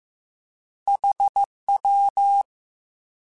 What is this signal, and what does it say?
Morse code, heard as a keyed tone, about 15 wpm, 800 Hz. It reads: HW
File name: Misc / Misc-mp3s / Words @13